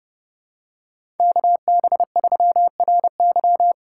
Morse code, heard as a keyed tone, about 30 wpm, 700 Hz. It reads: KB3RY